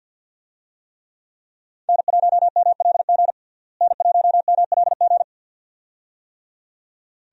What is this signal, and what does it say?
Morse code, heard as a keyed tone, about 47 wpm, 700 Hz. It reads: N1MLG N1MLG
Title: USA Call Signs - 5 Characters - Repeated 2x - 50wpm